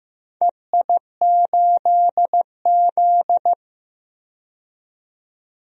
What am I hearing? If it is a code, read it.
Morse code, heard as a keyed tone, about 15 wpm, 700 Hz. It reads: EI8Z
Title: DXCC-100 Call Signs - 4 Characters - 15wpm